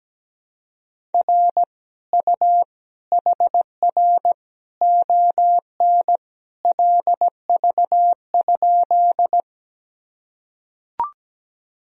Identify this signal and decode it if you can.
Morse code, heard as a keyed tone, about 17 wpm, 700 Hz. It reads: R U HR ON LV?
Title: Sentences from Top 400 Words and Abbreviations 17wpm